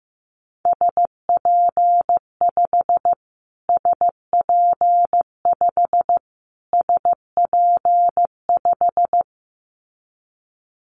Morse code, heard as a keyed tone, about 15 wpm, 700 Hz. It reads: SP5 SP5 SP5
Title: Callsign Prefixes for Top 25 DXCC - Repeated 3x - 15wpm